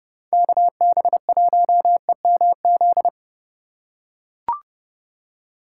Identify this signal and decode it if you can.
Morse code, heard as a keyed tone, about 30 wpm, 700 Hz. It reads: KB1EMZ
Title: USA Call Signs - 6 Characters - 4x Character Spacing - 30wpm